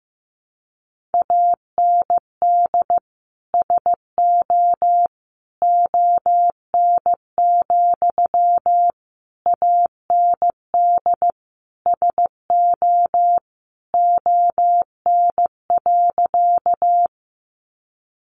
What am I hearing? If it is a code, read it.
Morse code, heard as a keyed tone, about 14 wpm, 700 Hz. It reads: AND SO ON, AND SO ON.
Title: Sentences from Top 100 Words 15wpm